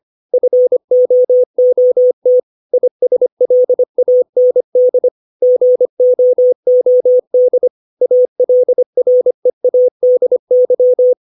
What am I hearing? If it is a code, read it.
Morse code, heard as a keyed tone, about 25 wpm, 500 Hz. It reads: FOOT ISLAND GOOD ALREADY
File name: 632 words / 632 Words 25WPM